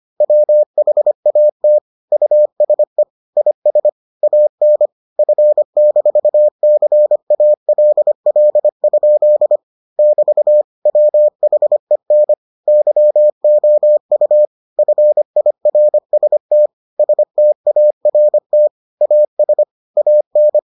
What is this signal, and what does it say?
Morse code, heard as a keyed tone, about 25 wpm, 600 Hz. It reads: WHAT USE IS AN F-CALL? <BT> WHEN YOU FIRST START AS AN